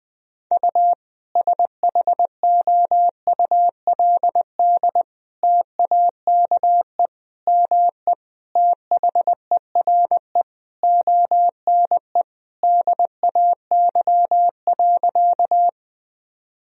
Morse code, heard as a keyed tone, about 21 wpm, 700 Hz. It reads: U SHOULD TAKE ME THERE ONE DAY.
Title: Sentences from Top 300 Words and Abbreviations 20wpm